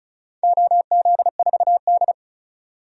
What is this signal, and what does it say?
Morse code, heard as a keyed tone, about 36 wpm, 700 Hz. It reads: OZ4D
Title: DXCC-100 Call Signs - 4 Characters - 35wpm